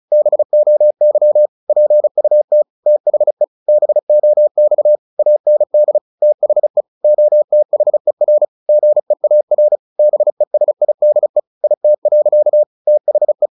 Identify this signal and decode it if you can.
Morse code, heard as a keyed tone, about 34 wpm, 600 Hz. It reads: BOY PUT THE BOX AND THE OTHER GEAR BESIDE IT. THE
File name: the old man and the sea / the old man and the sea 1 35wpm